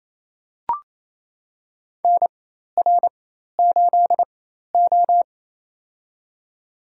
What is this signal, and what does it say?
Morse code, heard as a keyed tone, about 28 wpm, spaced as for 13 wpm, 700 Hz. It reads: NR8O
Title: USA Call Signs - 4 Characters - 4x Character Spacing - 28wpm